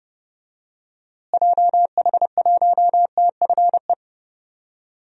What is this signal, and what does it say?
Morse code, heard as a keyed tone, about 32 wpm, 700 Hz. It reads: JH1TFE
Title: DXCC-100 Call Signs - 6 Characters - Repeated 2x - 30wpm